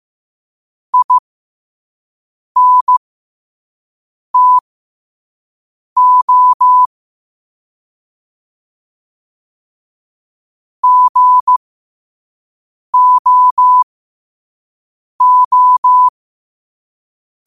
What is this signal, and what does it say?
Morse code, heard as a keyed tone, about 15 wpm, spaced as for 5 wpm, 1000 Hz. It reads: INTO GOO
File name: qotdfiles / 5wpm